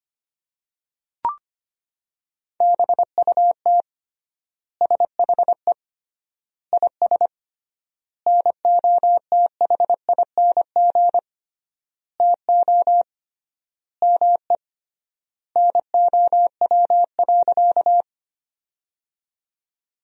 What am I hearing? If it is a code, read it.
Morse code, heard as a keyed tone, about 25 wpm, 700 Hz. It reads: BUT SHE IS NOTHING TO ME NOW.
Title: 3x Word-Spacing - Sentences from Top 500 Words - 25wpm